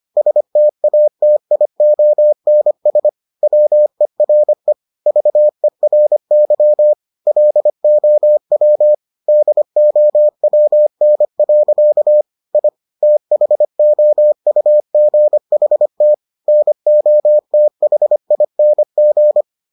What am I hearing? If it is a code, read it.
Morse code, heard as a keyed tone, about 25 wpm, 600 Hz. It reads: STATIONS WERE VERY LOW DOWN. I THOUGHT NOTHING